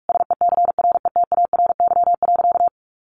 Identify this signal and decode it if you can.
Morse code, heard as a keyed tone, about 46 wpm, 700 Hz. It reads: SECRETARY.
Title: Sentences from Top 1300 Words 45wpm